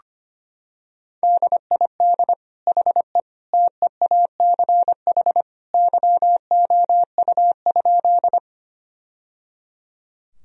Morse code, heard as a keyed tone, about 25 wpm, 700 Hz. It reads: DID HE TEACH YOU?
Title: Sentences from Top 1000 Words - Review Part 2 - 25wpm